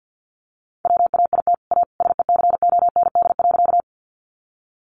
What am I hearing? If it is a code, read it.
Morse code, heard as a keyed tone, about 50 wpm, 700 Hz. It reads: WAIT A SECOND.
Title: Sentences from Top 500 Words 50wpm